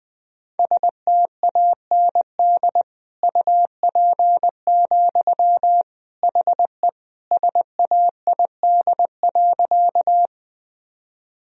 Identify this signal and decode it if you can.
Morse code, heard as a keyed tone, about 21 wpm, 700 Hz. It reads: STAND UP, HE SAID.
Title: Sentences from Top 1000 Words - Review Part 1 - 20wpm